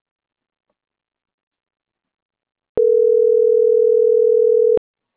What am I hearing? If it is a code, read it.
Ringback tone (North America), 440 + 480 Hz.